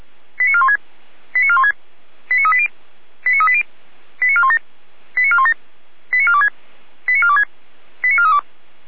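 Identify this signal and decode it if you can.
Five-tone selective call (ZVEI), heard as quick runs of five tones, tones of about 70 ms. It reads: 89416 89416 89380 89380 89417 89417 89426 89426 89432
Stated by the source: ZVEI five-tone selective call received on SDR.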